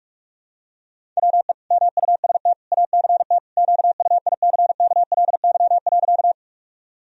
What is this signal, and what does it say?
Morse code, heard as a keyed tone, about 42 wpm, 700 Hz. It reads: WE MUST ACT QUICKLY.
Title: Sentences from Top 1000 Words - Review Part 1 - 45wpm